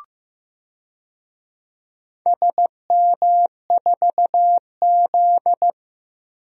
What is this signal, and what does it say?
Morse code, heard as a keyed tone, about 15 wpm, 700 Hz. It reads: SM4Z
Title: DXCC-100 Call Signs - 4 Characters - 15wpm